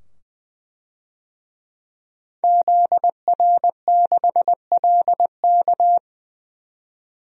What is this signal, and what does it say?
Morse code, heard as a keyed tone, about 20 wpm, 700 Hz. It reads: ZR6LK
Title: DXCC-100 Call Signs - 5 Characters - 20wpm